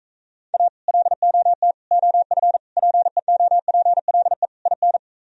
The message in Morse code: A LOT OF PEOPLE IN

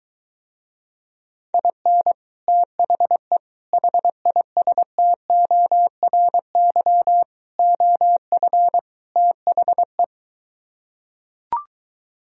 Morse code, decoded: IN THE HISTORY OF THE

